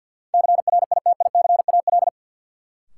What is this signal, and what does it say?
Morse code, heard as a keyed tone, about 52 wpm, 700 Hz. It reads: CRITICAL
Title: Sentences from Top 1100 Words - New Word Drill Alt - 50wpm